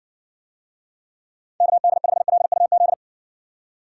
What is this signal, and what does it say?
Morse code, heard as a keyed tone, about 70 wpm, 700 Hz. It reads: KD5LUZ